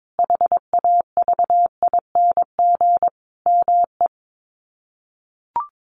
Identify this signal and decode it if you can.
Morse code, heard as a keyed tone, about 22 wpm, 700 Hz. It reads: HAVING ME
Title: Sets of 5 Words Encore 22wpm